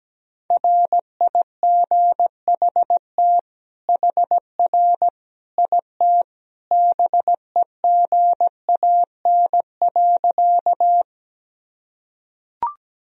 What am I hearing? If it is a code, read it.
Morse code, heard as a keyed tone, about 17 wpm, 700 Hz. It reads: RIGHT HR IT BEGAN.
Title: Sentences from Top 300 Words and Abbreviations 17wpm